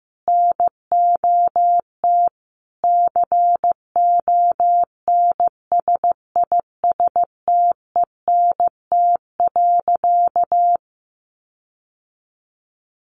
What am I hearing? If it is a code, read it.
Morse code, heard as a keyed tone, about 15 wpm, 700 Hz. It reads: NOT CONSISTENT.
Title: Sentences from Top 1500 Words 15wpm